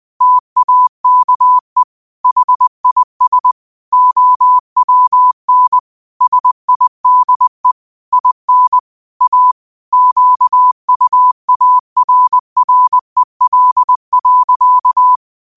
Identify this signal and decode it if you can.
Morse code, heard as a keyed tone, about 20 wpm, 1000 Hz. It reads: TAKE HIS OWN SIDE IN A QUARREL.
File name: qotdfiles / 20wpm